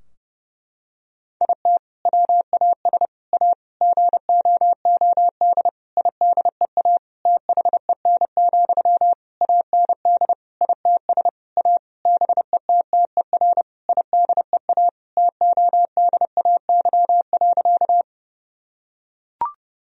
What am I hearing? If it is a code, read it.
Morse code, heard as a keyed tone, about 30 wpm, 700 Hz. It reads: IT WAS A GOOD IDEA THEN, AND ITS A BETTER IDEA TODAY.
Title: Sentences from Top 1000 Words - Review Part 4 - 30wpm